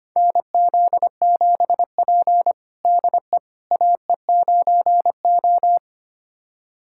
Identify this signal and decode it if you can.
Morse code, heard as a keyed tone, about 25 wpm, 700 Hz. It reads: NZ7P DE AE9O
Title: USA Callsign DE Callsign - 4 Characters - 25wpm